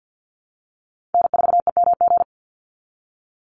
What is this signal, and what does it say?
Morse code, heard as a keyed tone, about 48 wpm, 700 Hz. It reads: N4EMG
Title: NA SOTA Chaser Callsigns 50wpm